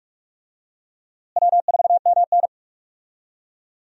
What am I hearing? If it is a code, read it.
Morse code, heard as a keyed tone, about 46 wpm, 700 Hz. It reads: W4MN
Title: USA Call Signs - 4 Characters - Repeated 3x - 45wpm